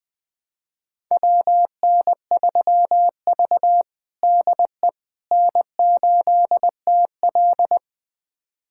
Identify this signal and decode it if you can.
Morse code, heard as a keyed tone, about 21 wpm, 700 Hz. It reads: WN3V DE N8TL
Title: USA Callsign DE Callsign - 4 Characters - 20wpm